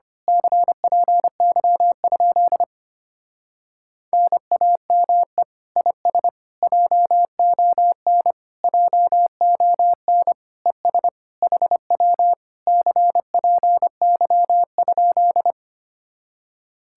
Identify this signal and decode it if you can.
Morse code, heard as a keyed tone, about 24 wpm, 700 Hz. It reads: CPY? NAME IS JON JON ES HW CPY?
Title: Speed-Racing Common QSO Phrases - 25wpm